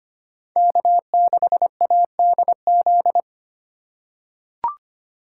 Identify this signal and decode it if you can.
Morse code, heard as a keyed tone, about 25 wpm, 700 Hz. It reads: K6ADZ
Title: USA Call Signs - 5 Characters - 3x Character Spacing - 25wpm